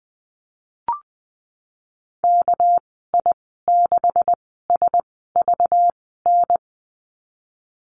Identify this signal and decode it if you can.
Morse code, heard as a keyed tone, about 20 wpm, spaced as for 15 wpm, 700 Hz. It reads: KI6SVN
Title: 213 Lesson 34g - Number 8 - Call Signs 20wpm